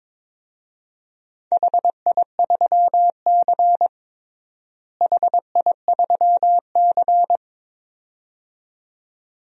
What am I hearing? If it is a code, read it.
Morse code, heard as a keyed tone, about 22 wpm, 700 Hz. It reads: HI3C HI3C
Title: DXCC-100 Call Signs - 4 Characters - Repeated 2x - 22wpm